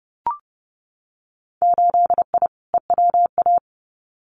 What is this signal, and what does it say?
Morse code, heard as a keyed tone, about 30 wpm, 700 Hz. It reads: 8I EWA
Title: Winter Field Day Exchanges Encore 30wpm